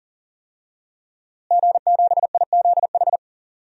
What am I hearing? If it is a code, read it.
Morse code, heard as a keyed tone, about 40 wpm, 700 Hz. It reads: G7IZH
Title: DXCC-100 Call Signs - 5 Characters - Repeated 2x - 40wpm